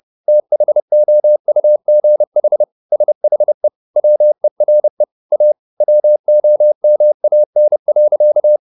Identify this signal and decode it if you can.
Morse code, heard as a keyed tone, about 30 wpm, 600 Hz. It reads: THOUGH SHE WERE A WOMAN.
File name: the old man and the sea / the old man and the sea 1 30wpm